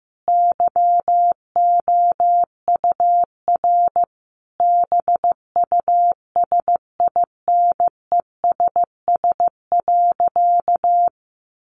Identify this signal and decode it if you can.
Morse code, heard as a keyed tone, about 15 wpm, 700 Hz. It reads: YOUR BUSINESS.
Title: Sentences from Top 600 Words 15wpm